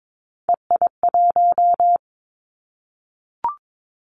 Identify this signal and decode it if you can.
Morse code, heard as a keyed tone, about 22 wpm, 700 Hz. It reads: EI1
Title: Callsign Prefixes for Top 50 DXCC - 22wpm